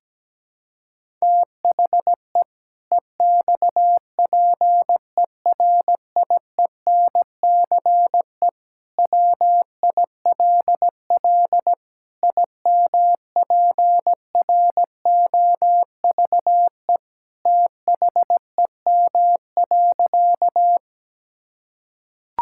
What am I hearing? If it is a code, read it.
Morse code, heard as a keyed tone, about 17 wpm, 700 Hz. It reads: THE EXPERIENCE WILL IMPROVE THEM.
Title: Sentences from Top 1000 Words - Review Part 5 - 17wpm